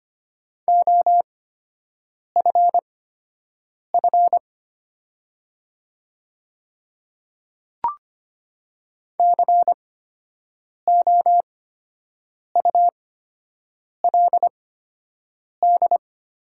Morse code, heard as a keyed tone, about 25 wpm, spaced as for 7 wpm, 700 Hz. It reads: OFF COULD
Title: 8x Character Spacing - Top 200 Words - 25wpm